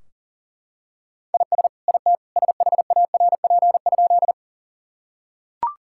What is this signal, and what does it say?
Morse code, heard as a keyed tone, about 40 wpm, 700 Hz. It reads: IS IT SHARP?
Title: Sentences from Top 1600 Words 40wpm